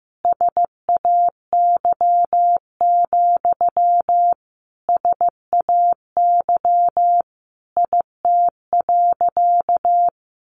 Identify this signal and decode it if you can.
Morse code, heard as a keyed tone, about 15 wpm, 700 Hz. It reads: SAY, SAY IT.
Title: Sentences from Top 1000 Words 15wpm